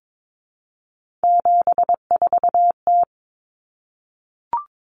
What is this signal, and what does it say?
Morse code, heard as a keyed tone, about 22 wpm, 700 Hz. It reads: 74T
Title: Common Cut Numbers - 3-Digit 22wpm